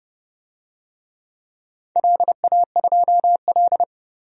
Morse code, heard as a keyed tone, about 30 wpm, 700 Hz. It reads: LA2L
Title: DXCC-100 Call Signs - 4 Characters - 30wpm